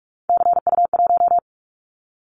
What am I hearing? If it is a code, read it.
Morse code, heard as a keyed tone, about 44 wpm, 700 Hz. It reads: CU1
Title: Callsign Prefixes for Top 100 DXCC - 45wpm